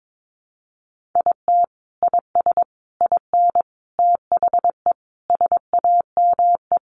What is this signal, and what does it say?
Morse code, heard as a keyed tone, about 22 wpm, 700 Hz. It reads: IT IS IN THE SAME
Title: Sentences from Top 1500 Words 22wpm